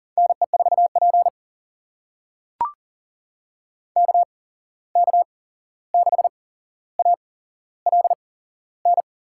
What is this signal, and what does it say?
Morse code, heard as a keyed tone, about 40 wpm, spaced as for 15 wpm, 700 Hz. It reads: NE4P KK6ALN
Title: USA Call Signs - 8x Character Spacing - 40wpm